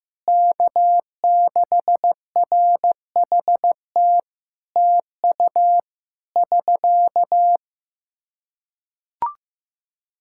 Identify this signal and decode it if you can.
Morse code, heard as a keyed tone, about 15 wpm, 700 Hz. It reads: K6RHT TU <SK>
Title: Practice QSO - Exchange 4 - Ending - 15wpm